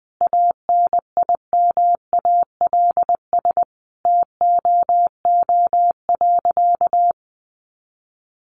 Morse code, heard as a keyed tone, about 20 wpm, 700 Hz. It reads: ANIMALS TOO.
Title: Easy Phrases 20wpm